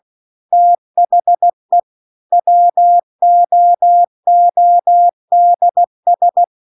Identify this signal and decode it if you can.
Morse code, heard as a keyed tone, about 16 wpm, 700 Hz. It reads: THE WOODS
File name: CW-The-Big-Twoheaded-River-2-16WPM